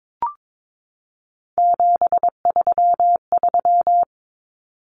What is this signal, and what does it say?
Morse code, heard as a keyed tone, about 22 wpm, 700 Hz. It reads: 733
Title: Common Cut Numbers - 3-Digit 22wpm